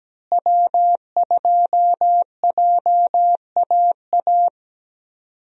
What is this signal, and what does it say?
Morse code, heard as a keyed tone, about 17 wpm, 700 Hz. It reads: W2JAA
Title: USA Call Signs - 5 Characters - Repeated 2x - 17wpm